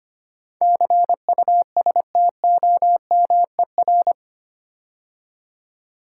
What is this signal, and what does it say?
Morse code, heard as a keyed tone, about 25 wpm, 700 Hz. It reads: CUSTOMER
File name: 8 Letter Words - First 3 Letters Given - 25wpm